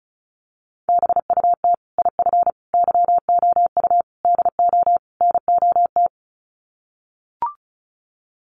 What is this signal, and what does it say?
Morse code, heard as a keyed tone, about 34 wpm, 700 Hz. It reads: BUT IF YOU DO NOT
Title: Sets of 5 Words Encore 35wpm